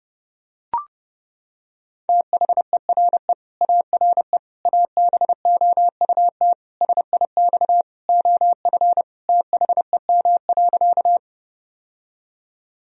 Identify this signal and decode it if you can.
Morse code, heard as a keyed tone, about 30 wpm, 700 Hz. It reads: THERE ARE ABOUT SIX OF THEM.
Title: Sentences from Top 1000 Words - Review Part 2 - 30wpm